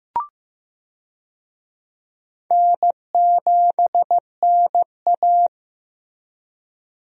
Morse code, heard as a keyed tone, about 15 wpm, 700 Hz. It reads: N7NA
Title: USA Call Signs - 4 Characters - Repeated 3x - 15wpm